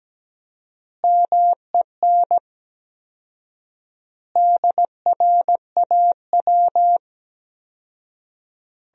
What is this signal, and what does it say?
Morse code, heard as a keyed tone, about 17 wpm, 700 Hz. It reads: MEN DRAW
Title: 4x Word-Spacing - Sentences from Top 300 Words - 17wpm